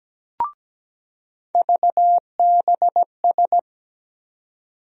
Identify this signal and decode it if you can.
Morse code, heard as a keyed tone, about 17 wpm, 700 Hz. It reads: VBS
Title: USA Call Sign Postfixes - 3 Letters 17wpm